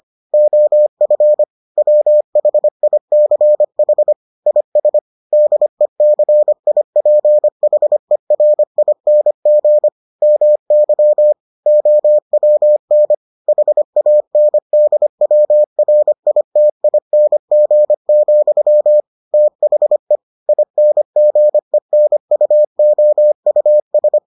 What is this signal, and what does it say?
Morse code, heard as a keyed tone, about 25 wpm, 600 Hz. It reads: OF WHICH IS DECIPHERING MY OWN HANDWRITING, THE INGENUOUS